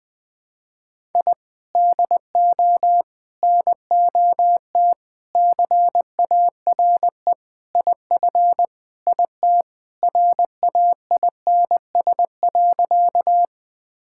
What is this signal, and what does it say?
Morse code, heard as a keyed tone, about 20 wpm, 700 Hz. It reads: I DO NOT CARE IF IT RAINS.
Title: Sentences with 5-Word Phrases - 20wpm